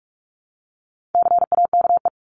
Morse code, heard as a keyed tone, about 44 wpm, 700 Hz. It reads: CAKE
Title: Top 1501-1750 Words 45wpm